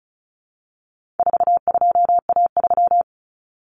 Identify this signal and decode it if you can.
Morse code, heard as a keyed tone, about 34 wpm, 700 Hz. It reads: 42A3
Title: Common Cut Numbers - 4-Digit 35wpm